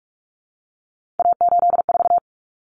Morse code, heard as a keyed tone, about 44 wpm, 700 Hz. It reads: A84
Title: Common Cut Numbers - 3-Digit 45wpm